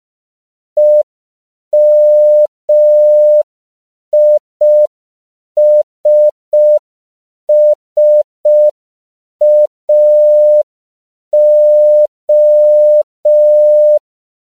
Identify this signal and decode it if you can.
Morse code, heard as a keyed tone, about 5 wpm, 600 Hz. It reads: EMISSAO
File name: CW / 5PPM